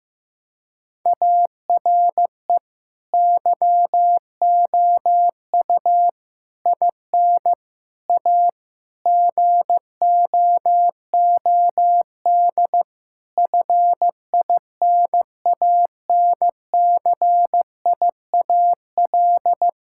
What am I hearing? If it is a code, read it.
Morse code, heard as a keyed tone, about 16 wpm, 700 Hz. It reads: ARE YOU IN A GOOD FINANCIAL
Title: Sentences from Top 1000 Words - Review Part 5 - 15wpm